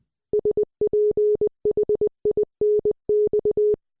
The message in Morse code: SPHINX